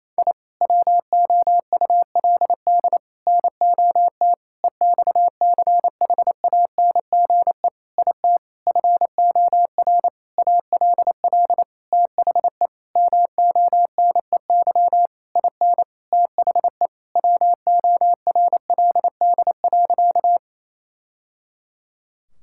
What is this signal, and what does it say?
Morse code, heard as a keyed tone, about 28 wpm, 700 Hz. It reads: I WOULD NOT EXCHANGE IT FOR ALL THE MONEY IN THE WORLD.
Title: Sentences from Top 1000 Words - Review Part 6 - 28wpm